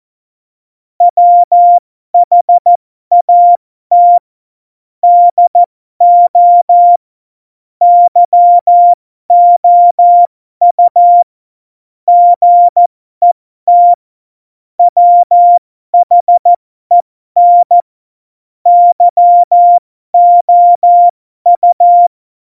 Morse code, heard as a keyed tone, about 14 wpm, spaced as for 12 wpm, 700 Hz. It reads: WHAT DO YOU GET WHEN YOU
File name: Morsekurs / Witze